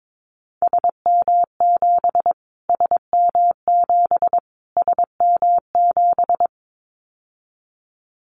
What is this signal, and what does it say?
Morse code, heard as a keyed tone, about 22 wpm, 700 Hz. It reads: SM7 SM7 SM7
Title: Callsign Prefixes for Top 25 DXCC - Repeated 3x - 22wpm